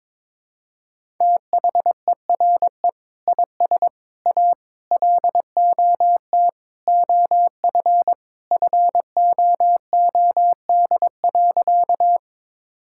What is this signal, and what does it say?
Morse code, heard as a keyed tone, about 22 wpm, 700 Hz. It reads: THERE IS A LOT OF FOOD.
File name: practice / 22wpm / Sentences with 3-Word Phrases - 22wpm